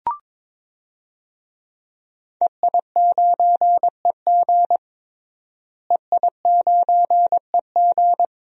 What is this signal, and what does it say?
Morse code, heard as a keyed tone, about 22 wpm, 700 Hz. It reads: EI9EG EI9EG
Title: DXCC-100 Call Signs - 5 Characters - Repeated 3x - 22wpm